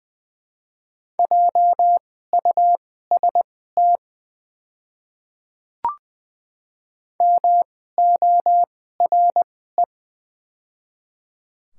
Morse code, heard as a keyed tone, about 21 wpm, spaced as for 15 wpm, 700 Hz. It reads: JUST MORE
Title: 2x Character Spacing - Top 200 Words - 20wpm